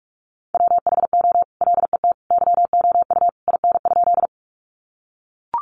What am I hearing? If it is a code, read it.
Morse code, heard as a keyed tone, about 44 wpm, 700 Hz. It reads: WHO LET YOU IN?
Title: Sentences from Top 300 Words 45wpm